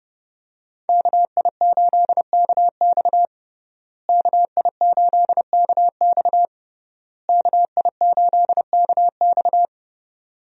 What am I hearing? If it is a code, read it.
Morse code, heard as a keyed tone, about 30 wpm, 700 Hz. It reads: KI8KX KI8KX KI8KX
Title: USA Call Signs - 5 Characters - Repeated 3x - 30wpm